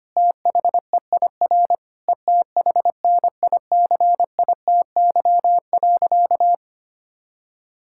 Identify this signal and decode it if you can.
Morse code, heard as a keyed tone, about 25 wpm, 700 Hz. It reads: THEIR ETHNICITY.
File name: practice / 25wpm / Sentences with 9-Word Phrases - 25wpm